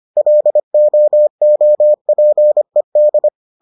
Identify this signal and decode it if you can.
Morse code, heard as a keyed tone, about 25 wpm, 600 Hz. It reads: LOOPED